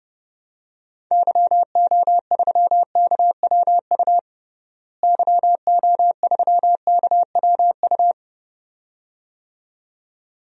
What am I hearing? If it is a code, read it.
Morse code, heard as a keyed tone, about 32 wpm, 700 Hz. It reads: YO3KWU YO3KWU
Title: DXCC-100 Call Signs - 6 Characters - Repeated 2x - 30wpm